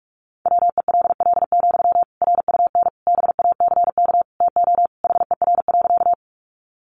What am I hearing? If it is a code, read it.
Morse code, heard as a keyed tone, about 44 wpm, 700 Hz. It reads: WELL, RUN BACK TO HER.
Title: Sentences from Top 300 Words 45wpm